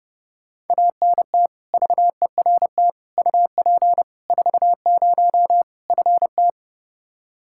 Morse code, heard as a keyed tone, about 32 wpm, 700 Hz. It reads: ANT VERT UP 40 FT
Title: Practice QSO - Exchange 2 - Info - 30wpm